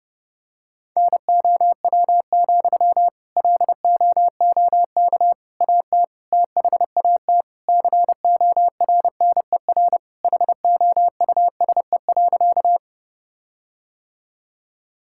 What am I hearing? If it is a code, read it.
Morse code, heard as a keyed tone, about 30 wpm, 700 Hz. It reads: NOW, LOOK AT THAT CORNER HOUSE.
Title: Sentences from Top 1600 Words 30wpm